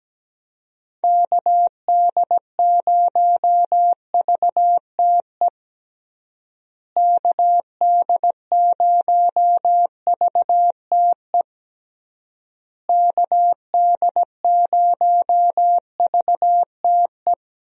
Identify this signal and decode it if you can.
Morse code, heard as a keyed tone, about 17 wpm, 700 Hz. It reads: KD0VTE KD0VTE KD0VTE
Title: USA Call Signs - 6 Characters - Repeated 3x - 17wpm